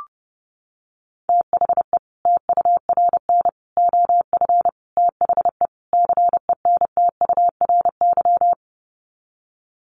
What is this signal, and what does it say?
Morse code, heard as a keyed tone, about 30 wpm, 700 Hz. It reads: THE TURN OF THE CENTURY
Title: Sets of 5 Words 30wpm